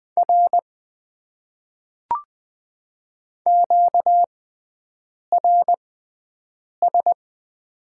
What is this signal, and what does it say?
Morse code, heard as a keyed tone, about 20 wpm, spaced as for 7 wpm, 700 Hz. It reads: R QRS
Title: Top 90 QSO Elements - 6x Character Spacing - 20wpm